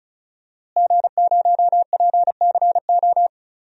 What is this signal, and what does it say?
Morse code, heard as a keyed tone, about 36 wpm, 700 Hz. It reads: G0PCO
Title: DXCC-100 Call Signs - 5 Characters - Repeated 3x - 35wpm